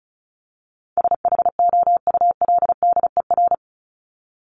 Morse code, heard as a keyed tone, about 34 wpm, 700 Hz. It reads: SHOULDER
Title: Top 1501-1750 Words 35wpm